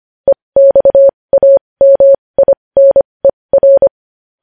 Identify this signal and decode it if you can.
Morse code, heard as a keyed tone, about 25 wpm, spaced as for 20 wpm, 550 Hz. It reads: EXAMINER
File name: podcast / News headlines-1x-25x20